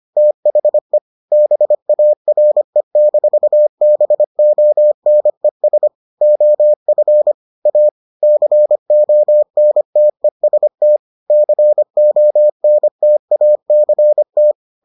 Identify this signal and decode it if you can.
Morse code, heard as a keyed tone, about 25 wpm, 600 Hz. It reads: THE BARE-BONES OF A CONTEST CONTACT